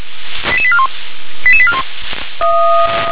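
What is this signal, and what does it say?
Five-tone selective call (ZVEI), heard as quick runs of five tones, tones of about 70 ms. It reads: 88841 88841
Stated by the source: ZVEI five-tone call received on an SDR.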